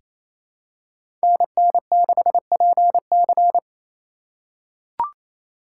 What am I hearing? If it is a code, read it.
Morse code, heard as a keyed tone, about 27 wpm, 700 Hz. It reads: NN6PC
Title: USA Call Signs - 5 Characters - 2x Character Spacing - 28wpm